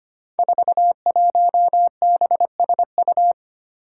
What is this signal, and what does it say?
Morse code, heard as a keyed tone, about 25 wpm, 700 Hz. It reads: 41BSU
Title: Five Challenging Letter-Number Combinations 25wpm